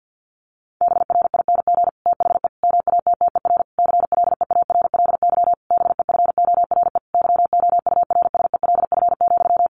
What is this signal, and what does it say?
Morse code, heard as a keyed tone, about 50 wpm, 700 Hz. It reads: BRING THE MATTER CLEARLY BEFORE YOURSELF,
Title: Sentences from Top 1300 Words 50wpm